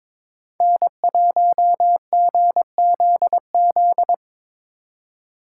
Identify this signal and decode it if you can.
Morse code, heard as a keyed tone, about 23 wpm, 700 Hz. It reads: N1GZZ
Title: USA Call Signs - 5 Characters - 3x Character Spacing - 22wpm